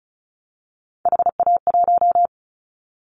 Morse code, heard as a keyed tone, about 34 wpm, 700 Hz. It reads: HA1
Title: Callsign Prefixes for Top 25 DXCC - 35wpm